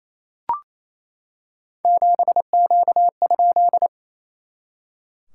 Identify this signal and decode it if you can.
Morse code, heard as a keyed tone, about 28 wpm, 700 Hz. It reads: 7Q?
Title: SOTA QSO Practice 28wpm